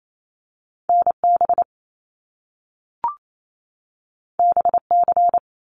Morse code, heard as a keyed tone, about 28 wpm, 700 Hz. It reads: NB BC
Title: Canadian Provinces and Territories 28wpm